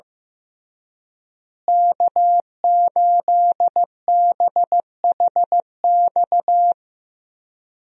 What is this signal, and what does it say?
Morse code, heard as a keyed tone, about 15 wpm, 700 Hz. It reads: K8BHX
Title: USA Call Signs - 5 Characters - Repeated 3x - 15wpm